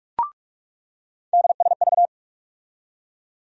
Morse code, heard as a keyed tone, about 41 wpm, 700 Hz. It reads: DSV
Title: USA Call Sign Postfixes - 3 Letters 45wpm